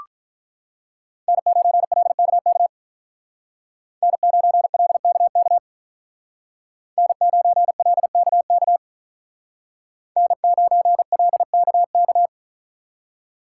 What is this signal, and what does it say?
Morse code, heard as a keyed tone, about 44 wpm, 700 Hz. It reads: N9LKK N9LKK N9LKK N9LKK